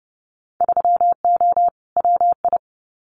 Morse code, heard as a keyed tone, about 30 wpm, 700 Hz. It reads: 3O WI
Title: Winter Field Day Exchanges 30wpm